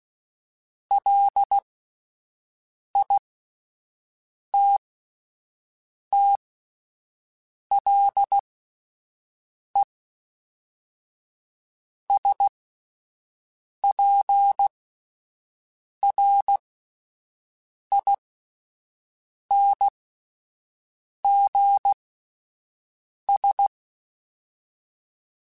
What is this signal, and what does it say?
Morse code, heard as a keyed tone, about 16 wpm, spaced as for 5 wpm, 800 Hz. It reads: LITTLESPRINGS